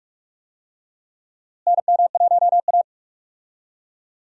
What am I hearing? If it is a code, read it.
Morse code, heard as a keyed tone, about 42 wpm, 700 Hz. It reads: NM1A
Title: USA Call Signs - 4 Characters - Repeated 2x - 45wpm